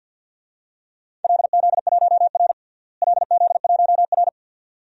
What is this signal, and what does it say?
Morse code, heard as a keyed tone, about 53 wpm, 700 Hz. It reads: LZ1R LZ1R
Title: DXCC-100 Call Signs - 4 Characters - Repeated 2x - 50wpm